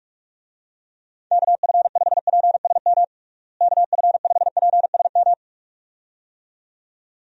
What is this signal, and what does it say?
Morse code, heard as a keyed tone, about 41 wpm, 700 Hz. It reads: KF5PSM KF5PSM
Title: USA Call Signs - 6 Characters - Repeated 2x - 45wpm